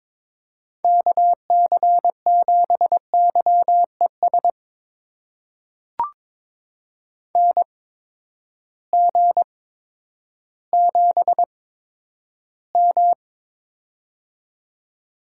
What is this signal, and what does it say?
Morse code, heard as a keyed tone, about 22 wpm, spaced as for 11 wpm, 700 Hz. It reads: KC7YES NG7M
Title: USA Call Signs - 8x Character Spacing - 22wpm